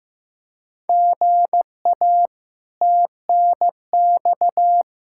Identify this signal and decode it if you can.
Morse code, heard as a keyed tone, about 16 wpm, 700 Hz. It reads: GA TNX
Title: Practice QSO - Exchange 1 - Beginning - 15wpm